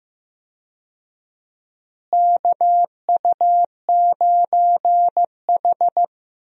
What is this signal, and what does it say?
Morse code, heard as a keyed tone, about 15 wpm, 700 Hz. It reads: KU9H
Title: USA Call Signs - 4 Characters - Repeated 2x - 15wpm